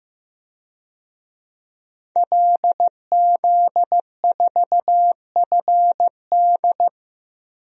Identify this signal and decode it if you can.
Morse code, heard as a keyed tone, about 15 wpm, 700 Hz. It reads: LZ4FD
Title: DXCC-100 Call Signs - 5 Characters - Repeated 3x - 15wpm